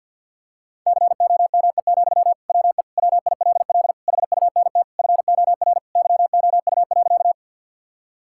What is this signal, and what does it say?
Morse code, heard as a keyed tone, about 50 wpm, 700 Hz. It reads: COME, WE WILL HUNT FOR YOU.
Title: Sentences from Top 1600 Words 50wpm